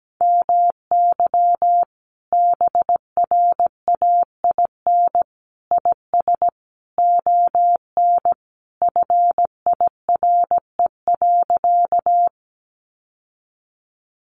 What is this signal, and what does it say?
Morse code, heard as a keyed tone, about 17 wpm, 700 Hz. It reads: MY BRAIN IS ON FIRE.
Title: Sentences from Top 1100 Words 17wpm